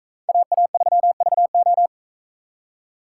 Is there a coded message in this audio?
Yes: Morse code from 0.3 to 1.9 s, heard as a keyed tone, about 40 wpm, 700 Hz. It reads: AA3VO